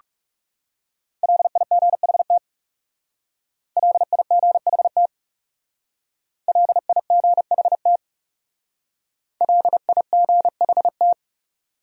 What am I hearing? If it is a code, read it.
Morse code, heard as a keyed tone, about 39 wpm, 700 Hz. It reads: LIGHT LIGHT LIGHT LIGHT